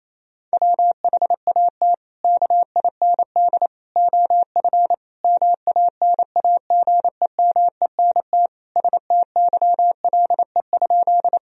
Morse code, heard as a keyed tone, about 28 wpm, 700 Hz. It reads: WHAT KIND OF MANAGEMENT STYLE?
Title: Sentences from Top 1000 Words - Review Part 4 - 28wpm